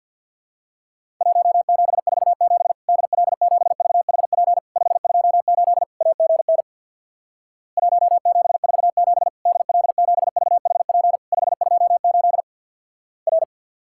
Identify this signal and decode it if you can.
Morse code, heard as a keyed tone, about 49 wpm, 700 Hz. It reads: 1747 DL7VHP 528 AGN 1747 DL7VHP 528 R
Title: Worked All Europe DX Contest - QTC Exchange with Repeats - 50wpm